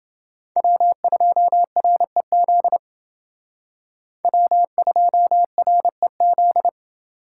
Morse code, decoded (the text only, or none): W2REZ W2REZ